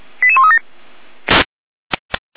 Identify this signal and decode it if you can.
Five-tone selective call (ZVEI), heard as quick runs of five tones, tones of about 70 ms. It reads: 88137